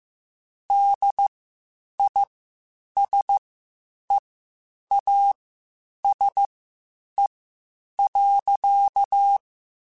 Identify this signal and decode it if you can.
Morse code, heard as a keyed tone, about 15 wpm, spaced as for 8 wpm, 800 Hz. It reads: DISEASE.
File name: Quotes Spaced / Quotes Spaced-mp3s / HoofInMouth sp @13